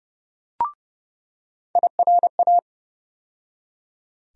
Morse code, heard as a keyed tone, about 30 wpm, 700 Hz. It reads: IRA
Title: USA Call Sign Postfixes - 3 Letters 30wpm